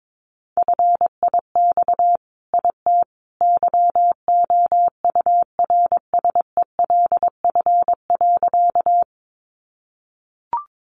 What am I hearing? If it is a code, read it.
Morse code, heard as a keyed tone, about 22 wpm, 700 Hz. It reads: FIX IT YOURSELF.
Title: Sentences from Top 1400 Words 22wpm